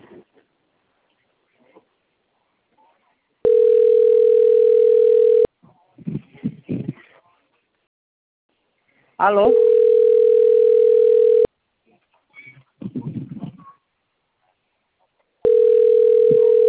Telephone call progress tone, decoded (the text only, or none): ringback tone